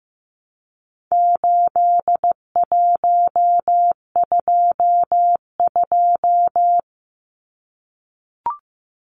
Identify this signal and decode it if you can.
Morse code, heard as a keyed tone, about 15 wpm, 700 Hz. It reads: 8122
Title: Numbers - 4-Digit 15wpm